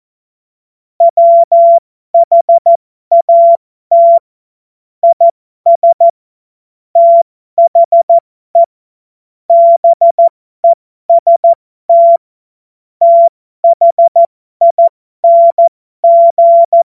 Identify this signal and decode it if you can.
Morse code, heard as a keyed tone, about 14 wpm, spaced as for 12 wpm, 680 Hz. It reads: WHAT IS THE BEST THING
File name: Morsekurs / Witze